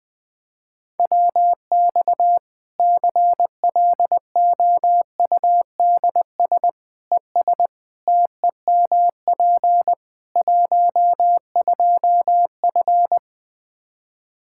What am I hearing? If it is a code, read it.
Morse code, heard as a keyed tone, about 20 wpm, 700 Hz. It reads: WX CLOUDS ES TEMP 12F
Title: Practice QSO - Exchange 2 - Info - 20wpm